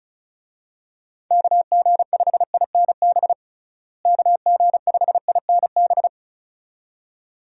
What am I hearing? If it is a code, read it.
Morse code, heard as a keyed tone, about 36 wpm, 700 Hz. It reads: KG5INB KG5INB
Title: USA Call Signs - 6 Characters - Repeated 2x - 35wpm